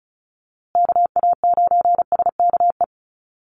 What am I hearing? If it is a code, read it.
Morse code, heard as a keyed tone, about 34 wpm, 700 Hz. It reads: KA9SKE
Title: USA Call Signs Repeated 2x 35wpm